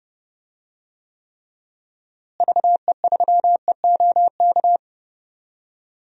Morse code, heard as a keyed tone, about 29 wpm, 700 Hz. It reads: VE3EOK
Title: DXCC-100 Call Signs - 6 Characters - Repeated 3x - 30wpm